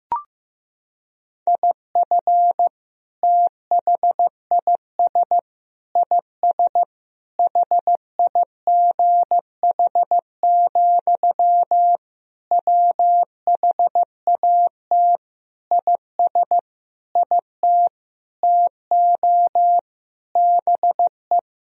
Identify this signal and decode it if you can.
Morse code, heard as a keyed tone, about 15 wpm, 700 Hz. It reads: IF THIS IS HIGH, WHAT IS IT TO BE
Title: Sentences from Top 1000 Words - Review Part 1 - 15wpm